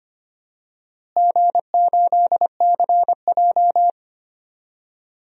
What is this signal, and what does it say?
Morse code, heard as a keyed tone, about 25 wpm, 700 Hz. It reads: G8CJ
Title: DXCC-100 Call Signs - 4 Characters - Repeated 2x - 25wpm